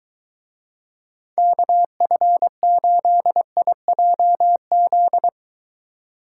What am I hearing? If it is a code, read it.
Morse code, heard as a keyed tone, about 23 wpm, 700 Hz. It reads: KF8IJZ